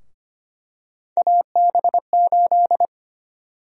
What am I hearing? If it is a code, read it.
Morse code, heard as a keyed tone, about 25 wpm, 700 Hz. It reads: AB8
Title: USA Call Sign Prefixes - 25wpm